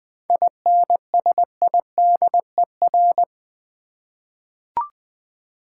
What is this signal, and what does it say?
Morse code, heard as a keyed tone, about 20 wpm, 700 Hz. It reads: INSIDER
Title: 7 Letter Words - First 3 Letters Given - 20wpm